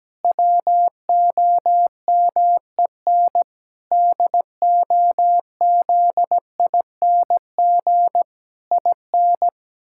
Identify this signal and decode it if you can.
Morse code, heard as a keyed tone, about 17 wpm, 700 Hz. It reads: WOMEN DOZING IN
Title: Pangram Challenge v2 - 17wpm